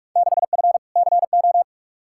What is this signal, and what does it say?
Morse code, heard as a keyed tone, about 41 wpm, 700 Hz. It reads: 6F CO
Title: Field Day Exchanges Encore - 45wpm